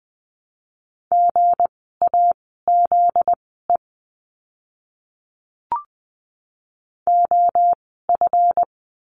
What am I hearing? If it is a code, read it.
Morse code, heard as a keyed tone, about 20 wpm, spaced as for 15 wpm, 700 Hz. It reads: GAZE OF
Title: 244 Lesson 39f - Letter z - Words 20wpm